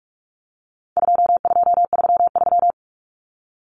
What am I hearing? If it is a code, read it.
Morse code, heard as a keyed tone, about 44 wpm, 700 Hz. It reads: 2233
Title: Common Cut Numbers - 4-Digit 45wpm